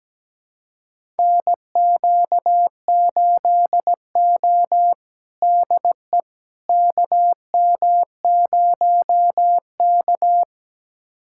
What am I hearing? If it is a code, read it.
Morse code, heard as a keyed tone, about 17 wpm, 700 Hz. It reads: NQ8O DE KM0K
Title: USA Callsign DE Callsign - 4 Characters - 17wpm